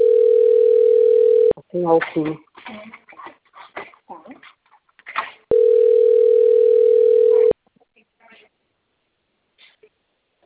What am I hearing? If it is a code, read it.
Ringback tone (North America), 440 + 480 Hz, with speech.
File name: rg-1600-unknown- - - .53732